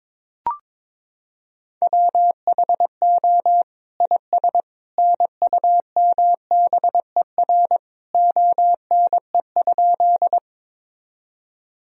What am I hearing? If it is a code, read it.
Morse code, heard as a keyed tone, about 22 wpm, 700 Hz. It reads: WHO IS NUMBER ONE?
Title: Sentences from Top 1000 Words - Review Part 1 - 22wpm